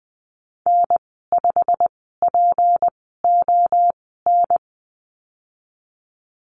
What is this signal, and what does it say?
Morse code, heard as a keyed tone, about 19 wpm, spaced as for 14 wpm, 700 Hz. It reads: N5PON
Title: 186 Lesson 30g - Letter b - Call Signs 20wpm